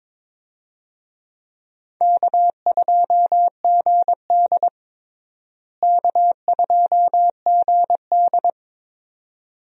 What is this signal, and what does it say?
Morse code, heard as a keyed tone, about 22 wpm, 700 Hz. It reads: K2GD K2GD
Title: USA Call Signs - 4 Characters - Repeated 2x - 22wpm